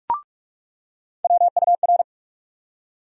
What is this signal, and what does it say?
Morse code, heard as a keyed tone, about 47 wpm, 700 Hz. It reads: WUR
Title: USA Call Sign Postfixes - 3 Letters 45wpm